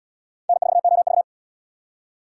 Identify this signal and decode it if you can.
Morse code, heard as a keyed tone, about 68 wpm, 700 Hz. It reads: N4OV